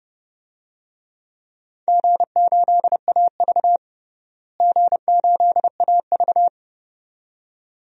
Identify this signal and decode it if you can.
Morse code, heard as a keyed tone, about 30 wpm, 700 Hz. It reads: G8AV G8AV
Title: DXCC-100 Call Signs - 4 Characters - Repeated 2x - 30wpm